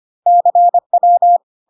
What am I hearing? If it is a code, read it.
Morse code, heard as a keyed tone, about 25 wpm, 700 Hz. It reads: CW
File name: CW abbreviations alphabetical repeated 3 times 25 25wpm